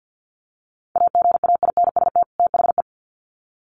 Morse code, heard as a keyed tone, about 50 wpm, 700 Hz. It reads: AGAINST THE
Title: Sets of 2 Words 50wpm